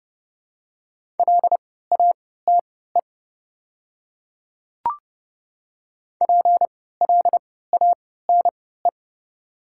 Morse code, heard as a keyed tone, about 30 wpm, spaced as for 17 wpm, 690 Hz. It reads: LATE PLANE
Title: 3x Character Spacing - Top 500 Words - 30wpm